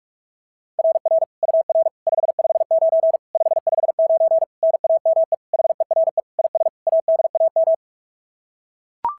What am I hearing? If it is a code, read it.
Morse code, heard as a keyed tone, about 44 wpm, 650 Hz. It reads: RR UR 559 559 NAME HERE IS ADAM
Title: SOTA QSO Practice 45wpm